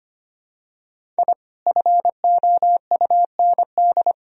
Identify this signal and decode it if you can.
Morse code, heard as a keyed tone, about 25 wpm, 700 Hz. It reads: I FOUND
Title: Sentences from Top 1000 Words - Review Part 1 - 25wpm